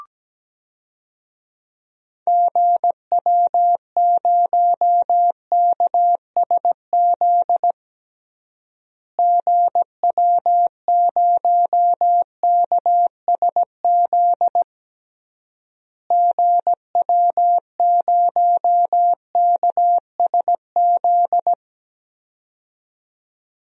Morse code, heard as a keyed tone, about 17 wpm, 700 Hz. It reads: GW0KSZ GW0KSZ GW0KSZ
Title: DXCC-100 Call Signs - 6 Characters - Repeated 3x - 17wpm